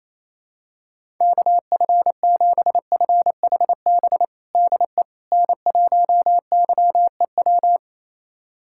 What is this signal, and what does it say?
Morse code, heard as a keyed tone, about 28 wpm, 700 Hz. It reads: KF7FHB DE N1YEW
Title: USA Callsign DE Callsign 28wpm